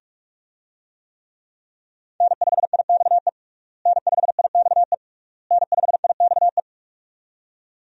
Morse code, heard as a keyed tone, about 46 wpm, 700 Hz. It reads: N5IXE N5IXE N5IXE
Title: USA Call Signs - 5 Characters - Repeated 3x - 45wpm